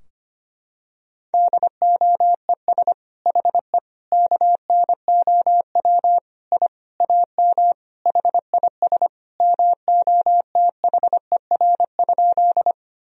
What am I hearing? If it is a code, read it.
Morse code, heard as a keyed tone, about 25 wpm, 700 Hz. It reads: DOES HE KNOW I AM HIS MOTHER?
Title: Sentences from Top 1000 Words - Review Part 2 - 25wpm